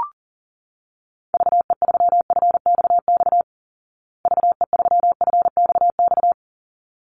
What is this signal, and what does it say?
Morse code, heard as a keyed tone, about 40 wpm, 700 Hz. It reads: VE3FXX VE3FXX
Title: Call Signs Repeated 3x - 40wpm